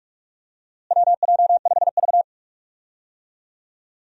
Morse code, heard as a keyed tone, about 45 wpm, 700 Hz. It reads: WJ5V
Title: USA Call Signs - 4 Characters - Repeated 3x - 45wpm